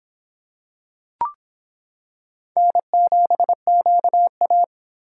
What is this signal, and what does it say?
Morse code, heard as a keyed tone, about 26 wpm, 700 Hz. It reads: N7QA